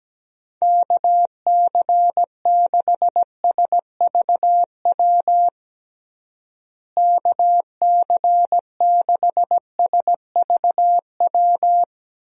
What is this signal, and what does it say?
Morse code, heard as a keyed tone, about 17 wpm, 700 Hz. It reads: KC6SVW KC6SVW
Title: USA Call Signs - 6 Characters - Repeated 2x - 17wpm